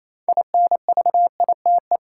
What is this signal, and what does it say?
Morse code, heard as a keyed tone, about 27 wpm, 700 Hz. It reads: INVITE
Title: Six Letter Words - 28wpm